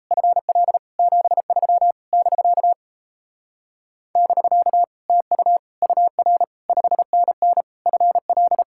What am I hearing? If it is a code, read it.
Morse code, heard as a keyed tone, about 33 wpm, 700 Hz. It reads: FL 73 <BK> <BK> TU UR 5NN FL
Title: Speed-Racing Common QSO Phrases Encore - 28wpm